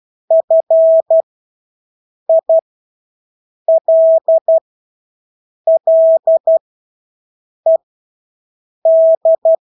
Morse code, heard as a keyed tone, about 12 wpm, spaced as for 6 wpm, 650 Hz. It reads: FILLED
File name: Engelske ord 12-6 wpm